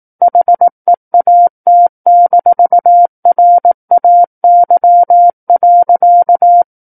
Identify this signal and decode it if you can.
Morse code, heard as a keyed tone, about 18 wpm, 700 Hz. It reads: HEAT-RAY.
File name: Book One / 09-Chapter 9 - 18 WPM